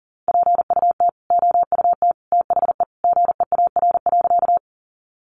Morse code, heard as a keyed tone, about 40 wpm, 700 Hz. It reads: PUT OUT THE GEAR.
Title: Sentences from Top 1400 Words 40wpm